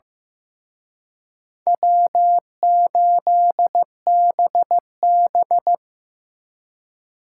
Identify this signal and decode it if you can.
Morse code, heard as a keyed tone, about 18 wpm, 700 Hz. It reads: W8BB
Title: Speed Racing - USA Call Signs 15wpm